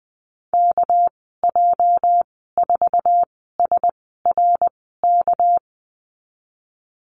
Morse code, heard as a keyed tone, about 20 wpm, spaced as for 14 wpm, 700 Hz. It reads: KJ4SRK
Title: 219 Lesson 35g - Number 0 - Call Signs 20wpm